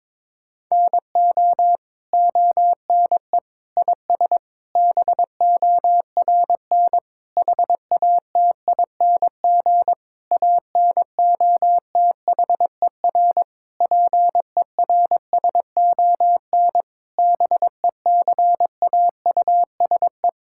Morse code, heard as a keyed tone, about 22 wpm, 700 Hz. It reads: NO ONE IS BORN HATING ANOTHER PERSON BECAUSE
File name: Sentences with 9-Word Phrases - 22wpm